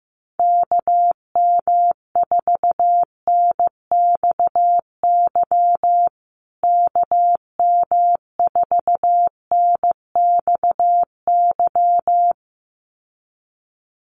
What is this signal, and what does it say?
Morse code, heard as a keyed tone, about 15 wpm, 700 Hz. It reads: KM4NXY KM4NXY
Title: USA Call Signs Repeated 2x 15wpm